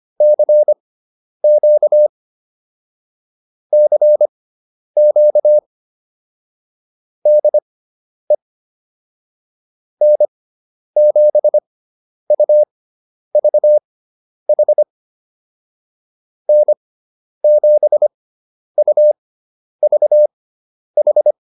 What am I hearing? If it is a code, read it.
Morse code, heard as a keyed tone, about 25 wpm, spaced as for 10 wpm, 600 Hz. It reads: CQ CQ DE N7UVH N7UVH